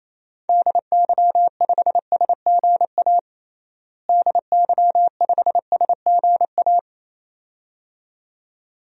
Morse code, heard as a keyed tone, about 28 wpm, 700 Hz. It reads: DY5SGA DY5SGA
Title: DXCC-100 Call Signs - 6 Characters - Repeated 2x - 28wpm